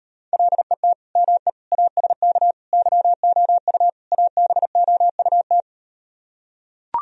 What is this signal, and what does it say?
Morse code, heard as a keyed tone, about 38 wpm, 700 Hz. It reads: LET ME ASK YOU ABOUT